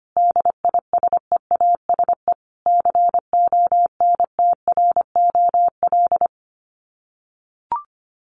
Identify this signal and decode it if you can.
Morse code, heard as a keyed tone, about 25 wpm, 700 Hz. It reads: DISEASE CONTROL
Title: Sets of 5 Words Encore2 25wpm